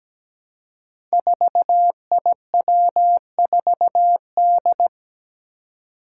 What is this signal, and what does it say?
Morse code, heard as a keyed tone, about 17 wpm, 700 Hz. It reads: 4IW4D
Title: Five Challenging Letter-Number Combinations 17wpm